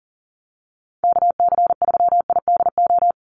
Morse code, heard as a keyed tone, about 40 wpm, 700 Hz. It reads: KC3IDO
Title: USA Call Signs 40wpm